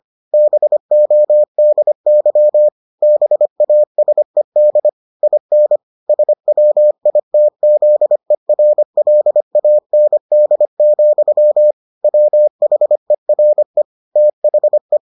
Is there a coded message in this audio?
Yes: Morse code from 0.3 to 15.0 s, heard as a keyed tone, about 25 wpm, 600 Hz. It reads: BODY BASED IN SWITZERLAND, WHERE THE